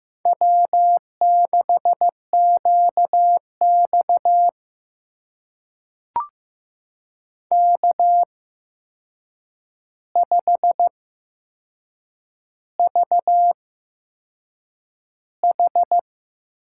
Morse code, heard as a keyed tone, about 15 wpm, spaced as for 6 wpm, 700 Hz. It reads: W6QX K5VH
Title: USA Call Signs - 8x Character Spacing - 15wpm